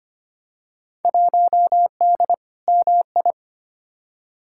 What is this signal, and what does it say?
Morse code, heard as a keyed tone, about 25 wpm, 700 Hz. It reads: 1D MI
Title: Field Day Exchanges - 25wpm